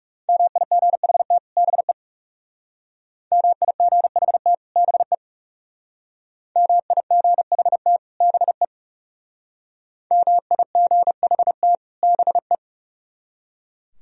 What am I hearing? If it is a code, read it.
Morse code, heard as a keyed tone, about 36 wpm, 700 Hz. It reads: MIGHT BE MIGHT BE MIGHT BE MIGHT BE
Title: Speed-Racing - Sets of 2 Words 30wpm